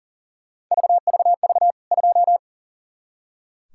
Morse code, heard as a keyed tone, about 40 wpm, 700 Hz. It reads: VVV 2
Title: K1USN SST QSO Practice 40wpm